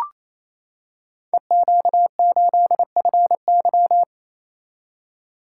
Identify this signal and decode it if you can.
Morse code, heard as a keyed tone, about 29 wpm, 700 Hz. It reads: EQ8FY
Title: Code Groups - 5 Letter:Numbers - 28wpm